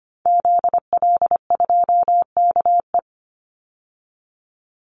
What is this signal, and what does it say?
Morse code, heard as a keyed tone, about 25 wpm, 700 Hz. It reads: ZL2KE
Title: Call Signs - 25wpm